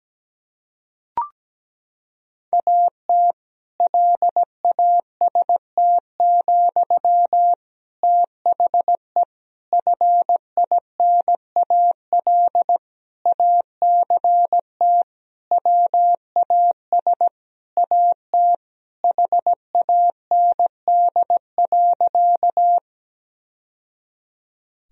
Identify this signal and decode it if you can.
Morse code, heard as a keyed tone, about 17 wpm, 700 Hz. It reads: AT LAST, THE FINAL ACT WAS AT HAND.
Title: Sentences from Top 1000 Words - Review Part 2 - 17wpm